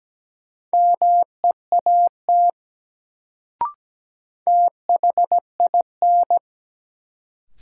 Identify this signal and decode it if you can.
Morse code, heard as a keyed tone, about 17 wpm, 700 Hz. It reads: MEAT THIN
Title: Sentences from Top 1200 Words - New Word Drill Alt - 17wpm